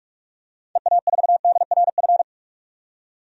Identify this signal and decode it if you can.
Morse code, heard as a keyed tone, about 42 wpm, 700 Hz. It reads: EA4DRF
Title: DXCC-100 Call Signs - 6 Characters - 45wpm